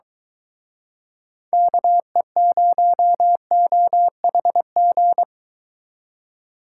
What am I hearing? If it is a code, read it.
Morse code, heard as a keyed tone, about 23 wpm, 700 Hz. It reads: KE0OHG